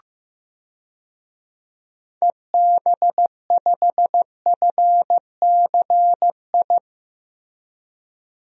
Morse code, heard as a keyed tone, about 15 wpm, 700 Hz. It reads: EB5FCI
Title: DXCC-100 Call Signs - 6 Characters - Repeated 3x - 15wpm